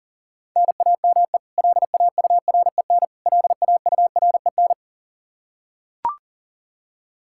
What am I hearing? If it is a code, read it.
Morse code, heard as a keyed tone, about 40 wpm, 700 Hz. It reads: NAME LAUREN LAUREN
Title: Practice QSO - Exchange 1 - Beginning - 40wpm